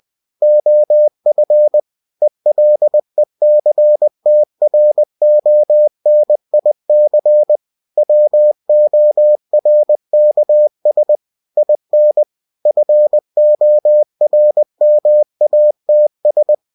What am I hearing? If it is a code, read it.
Morse code, heard as a keyed tone, about 20 wpm, 600 Hz. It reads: OF ELECTRONIC WORKS IN FORMATS
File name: cw-books / winnie-the-pooh / winnie-20wpm-page-0120